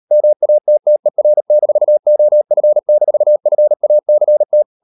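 Morse code, heard as a keyed tone, about 38 wpm, 600 Hz. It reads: MATTER-OF-FACT
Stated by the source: Morse code practice at 38 WPM, sending Ken Follett's novel Never in English, from CW med Gnister.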